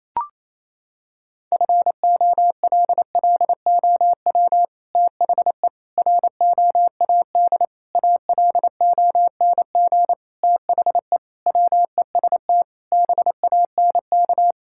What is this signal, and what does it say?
Morse code, heard as a keyed tone, about 28 wpm, 700 Hz. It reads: FOLLOW THE ROAD ALONG THE WEST BANK
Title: Sentences from Top 1000 Words - Review Part 6 - 28wpm